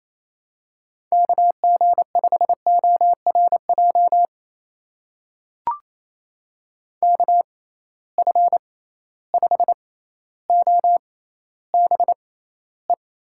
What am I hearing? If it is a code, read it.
Morse code, heard as a keyed tone, about 28 wpm, spaced as for 14 wpm, 700 Hz. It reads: KG5ORJ KF5OBE
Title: USA Call Signs - 6 Characters - 6x Character Spacing - 28wpm